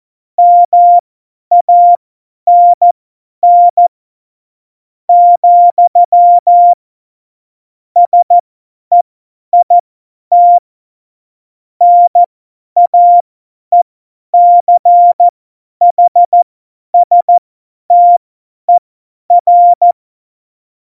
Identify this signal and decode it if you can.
Morse code, heard as a keyed tone, about 14 wpm, spaced as for 10 wpm, 700 Hz. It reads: MANN, SEIT NAECHSTER
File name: Morsekurs / Witze / Witz 20 March 2018, 50 F70